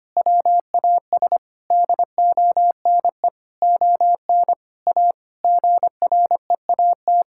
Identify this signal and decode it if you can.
Morse code, heard as a keyed tone, about 25 wpm, 700 Hz. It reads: WAS DONE ON A GREAT
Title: Sentences from Top 1000 Words - Review Part 5 - 25wpm